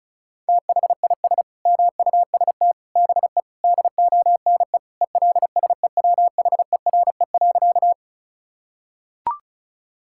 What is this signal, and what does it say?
Morse code, heard as a keyed tone, about 36 wpm, 700 Hz. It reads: THIS MUST BE DONE ELSEWHERE.
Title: Sentences from Top 1600 Words 35wpm